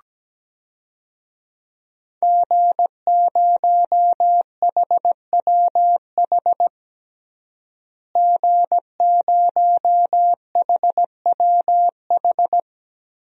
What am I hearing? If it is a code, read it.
Morse code, heard as a keyed tone, about 17 wpm, 700 Hz. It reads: G0HWH G0HWH
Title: DXCC-100 Call Signs - 5 Characters - Repeated 3x - 17wpm